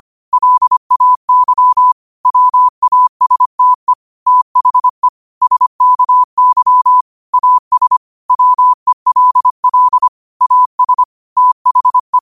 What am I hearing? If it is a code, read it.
Morse code, heard as a keyed tone, about 25 wpm, 1000 Hz. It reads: LAY WASTE THE SKY AS WELL AS THE
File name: qotdfiles / 25wpm